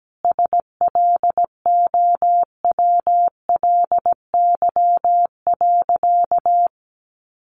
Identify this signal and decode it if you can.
Morse code, heard as a keyed tone, about 17 wpm, 700 Hz. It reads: SLOWLY.
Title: Sentences from Top 1100 Words 17wpm